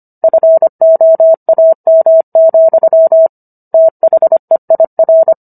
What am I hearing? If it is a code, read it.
Morse code, heard as a keyed tone, about 25 wpm, 650 Hz. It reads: FOAM, THEIR
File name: Book One / 16-Chapter 16 - 25 WPM